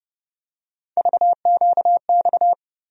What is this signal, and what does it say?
Morse code, heard as a keyed tone, about 30 wpm, 700 Hz. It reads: VQX
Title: USA Call Sign Postfixes - 3 Letters 30wpm